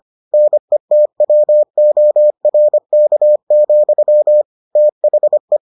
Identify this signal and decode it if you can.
Morse code, heard as a keyed tone, about 25 wpm, 600 Hz. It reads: NETWORK, THE